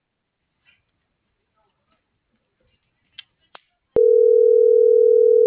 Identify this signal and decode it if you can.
Ringback tone (North America), 440 + 480 Hz.